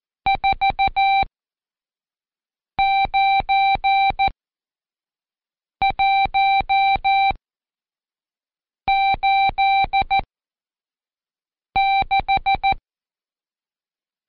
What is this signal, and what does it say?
Morse code, heard as a keyed tone, about 14 wpm, spaced as for 5 wpm, 780 Hz. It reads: 49186
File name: Code Quick / AudioDownld2 / Seg91-Day19-Review Numb